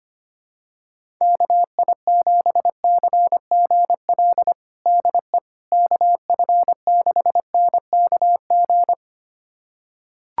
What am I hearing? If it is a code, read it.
Morse code, heard as a keyed tone, about 25 wpm, 700 Hz. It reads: KI7CGL DE KF6NKG
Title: USA Callsign DE Callsign - 6 Characters - 25wpm